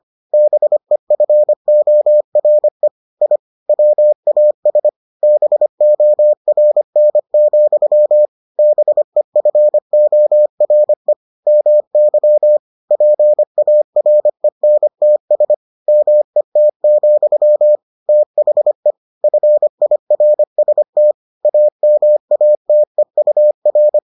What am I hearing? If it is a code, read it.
Morse code, heard as a keyed tone, about 25 wpm, 600 Hz. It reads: BEFORE I WAS BORN, BEFORE MY PARENTS MET, THE FIRST AMATEUR